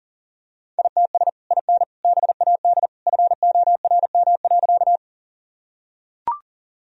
Morse code, heard as a keyed tone, about 40 wpm, 700 Hz. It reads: ITS IN BAD FORM.
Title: Sentences from Top 1000 Words - Review Part 1 - 40wpm